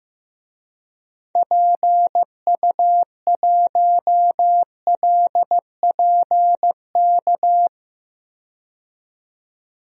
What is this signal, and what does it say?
Morse code, heard as a keyed tone, about 16 wpm, 700 Hz. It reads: PU1LPK
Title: DXCC-100 Call Signs - 6 Characters - 15wpm